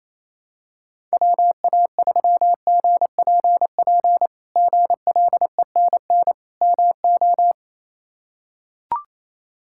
Morse code, heard as a keyed tone, about 28 wpm, 700 Hz. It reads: WA3GPP GLENN MO
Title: K1USN SST QSO Practice - 28wpm